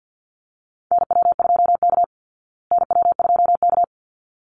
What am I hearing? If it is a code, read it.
Morse code, heard as a keyed tone, about 50 wpm, 700 Hz. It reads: NW2K NW2K
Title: Call Signs Repeated 3x - 50wpm